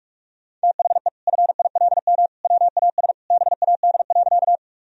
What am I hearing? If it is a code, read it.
Morse code, heard as a keyed tone, about 46 wpm, 700 Hz. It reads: THE FILM WAS BAD.
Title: Sentences from Top 1000 Words - Review Part 3 - 45wpm